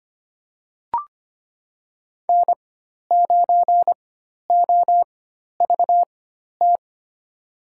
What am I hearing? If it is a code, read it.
Morse code, heard as a keyed tone, about 25 wpm, spaced as for 12 wpm, 700 Hz. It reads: N9OVT
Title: USA Call Signs - 5 Characters - 4x Character Spacing - 25wpm